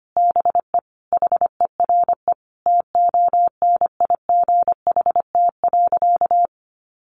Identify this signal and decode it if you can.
Morse code, heard as a keyed tone, about 25 wpm, 700 Hz. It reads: BE HERE TONIGHT.
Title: Sentences from Top 600 Words 25wpm